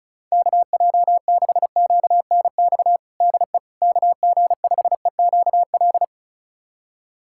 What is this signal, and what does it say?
Morse code, heard as a keyed tone, about 36 wpm, 700 Hz. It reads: KJ6QNX DE KG5EQL
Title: USA Callsign DE Callsign - 6 Characters - 35wpm